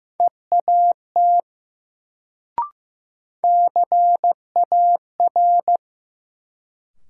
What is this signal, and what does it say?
Morse code, heard as a keyed tone, about 15 wpm, 700 Hz. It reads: EAT CAR
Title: Sentences from Top 300 Words - New Word Drill - 15wpm